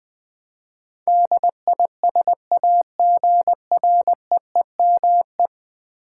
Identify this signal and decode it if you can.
Morse code, heard as a keyed tone, about 20 wpm, 700 Hz. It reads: DISAGREEME
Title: Build A Word - 5 Letters At A Time - 12-Letter Words - 20wpm